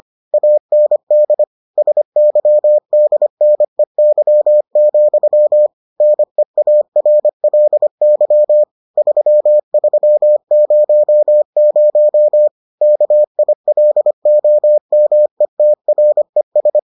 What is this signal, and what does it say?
Morse code, heard as a keyed tone, about 25 wpm, 600 Hz. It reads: AND SYDNEY, NEARLY 3300 KILOMETRES